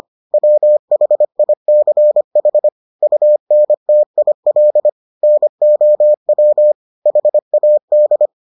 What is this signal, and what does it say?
Morse code, heard as a keyed tone, about 25 wpm, 600 Hz. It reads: WHICH UNTIL NOW HAD